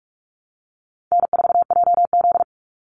Morse code, heard as a keyed tone, about 44 wpm, 700 Hz. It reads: N4JZ
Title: USA Call Signs 45wpm